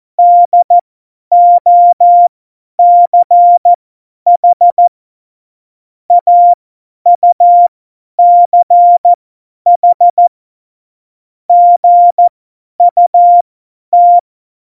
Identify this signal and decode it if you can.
Morse code, heard as a keyed tone, about 14 wpm, spaced as for 10 wpm, 700 Hz. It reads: DOCH AUCH GUT
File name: Morsekurs / Witze / Witz 24.Dezember 2019, 50 F70